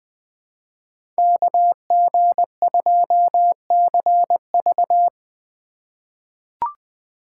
Morse code, heard as a keyed tone, about 20 wpm, 700 Hz. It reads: KG2CV
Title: 2023 POTA Hunter Call Signs - 20wpm